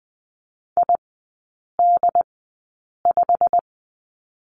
Morse code, heard as a keyed tone, about 20 wpm, spaced as for 8 wpm, 690 Hz. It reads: ID5
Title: 087 Lesson 15e - Number 5 - Three Characters 20wpm